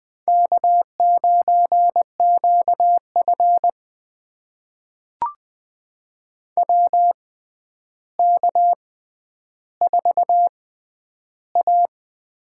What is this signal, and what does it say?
Morse code, heard as a keyed tone, about 20 wpm, spaced as for 10 wpm, 700 Hz. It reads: K9QF WK4A
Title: USA Call Signs - 4 Characters - 6x Character Spacing - 20wpm